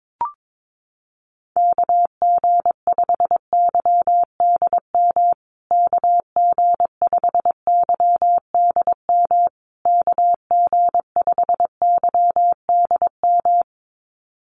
Morse code, heard as a keyed tone, about 22 wpm, 700 Hz. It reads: KG5YDM KG5YDM KG5YDM
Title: USA Call Signs Repeated 3x 22wpm